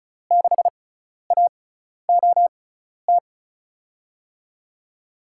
Morse code, heard as a keyed tone, about 34 wpm, spaced as for 12 wpm, 700 Hz. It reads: 6AOT
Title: USA Call Signs - 5 Characters - 6x Character Spacing - 35wpm